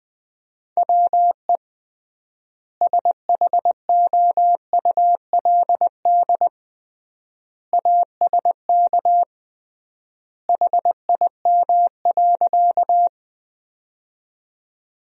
Morse code, heard as a keyed tone, about 21 wpm, 700 Hz. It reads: WE SHOULD ASK HIM.
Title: 3x Word-Spacing - Sentences from Top 300 Words - 20wpm